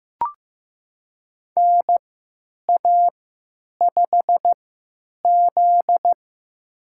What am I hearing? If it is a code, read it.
Morse code, heard as a keyed tone, about 15 wpm, spaced as for 9 wpm, 700 Hz. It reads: NA5Z
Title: USA Call Signs - 4 Characters - 3x Character Spacing - 15wpm